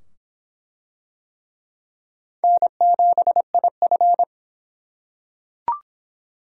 Morse code, heard as a keyed tone, about 23 wpm, 700 Hz. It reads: N7IF